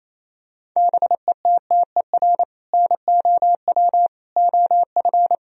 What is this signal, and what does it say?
Morse code, heard as a keyed tone, about 28 wpm, 700 Hz. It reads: BETTER NOW OF
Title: Sentences from Top 1000 Words - Review Part 3 - 28wpm